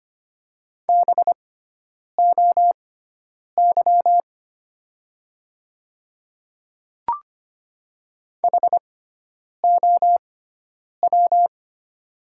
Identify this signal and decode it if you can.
Morse code, heard as a keyed tone, about 25 wpm, spaced as for 9 wpm, 700 Hz. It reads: BOY HOW
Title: 6x Character Spacing - Top 200 Words - 25wpm